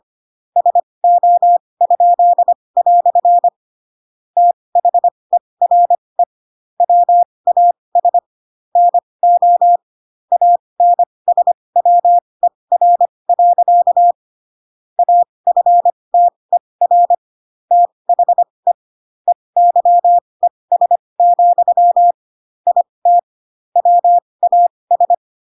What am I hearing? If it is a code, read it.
Morse code, heard as a keyed tone, about 25 wpm, spaced as for 20 wpm, 700 Hz. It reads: SO?" THERE WAS NO ANSWER. AFTER THE EYES, IT WAS
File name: Pinocchio / Pinocchio ch3 15 25 ik1hge